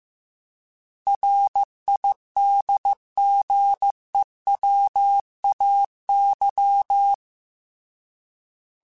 Morse code, heard as a keyed tone, about 15 wpm, 800 Hz. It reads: RIDGEWAY
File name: fromSMs / FromSMs mp3s / Cities@13